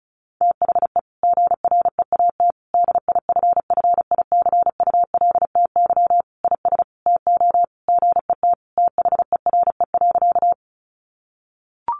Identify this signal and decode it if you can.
Morse code, heard as a keyed tone, about 34 wpm, 700 Hz. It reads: THE GREAT DIFFICULTY IS TO GET THERE.
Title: Sentences from Top 1500 Words 35wpm